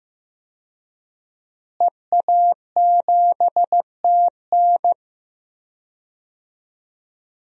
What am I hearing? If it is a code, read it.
Morse code, heard as a keyed tone, about 15 wpm, 700 Hz. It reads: EA7TN
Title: DXCC-100 Call Signs - 5 Characters - Repeated 2x - 15wpm